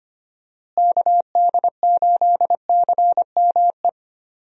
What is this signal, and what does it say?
Morse code, heard as a keyed tone, about 25 wpm, 700 Hz. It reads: KD8CME
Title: USA Call Signs - 6 Characters - Repeated 3x - 25wpm